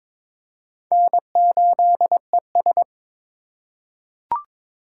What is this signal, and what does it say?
Morse code, heard as a keyed tone, about 22 wpm, 700 Hz. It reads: N8ES
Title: USA Call Signs - 4 Characters - 2x Character Spacing - 22wpm